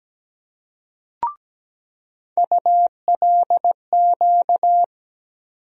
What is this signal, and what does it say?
Morse code, heard as a keyed tone, about 17 wpm, 700 Hz. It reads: ULQ
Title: USA Call Sign Postfixes - 3 Letters 17wpm